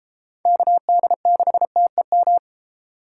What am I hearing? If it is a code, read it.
Morse code, heard as a keyed tone, about 31 wpm, 700 Hz. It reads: KD6TEM